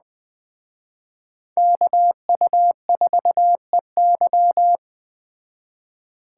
Morse code, heard as a keyed tone, about 20 wpm, 700 Hz. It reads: KU4EY
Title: Speed Racing - USA 2x2 Call Signs 17wpm